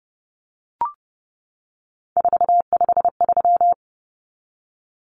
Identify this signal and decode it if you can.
Morse code, heard as a keyed tone, about 30 wpm, 700 Hz. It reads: 453
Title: Common Cut Numbers - 3-Digit 30wpm